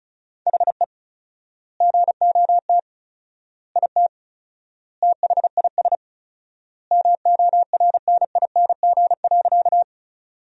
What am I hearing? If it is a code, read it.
Morse code, heard as a keyed tone, about 36 wpm, 700 Hz. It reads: HE GOT IT THIS MORNING.
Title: 4x Word-Spacing - Sentences from Top 400 Words - 35wpm